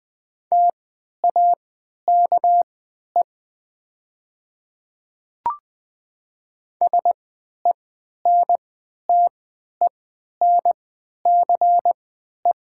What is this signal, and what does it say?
Morse code, heard as a keyed tone, about 20 wpm, spaced as for 11 wpm, 700 Hz. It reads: TAKE SENTENCE
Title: 3x Character Spacing - Top 200 Words - 20wpm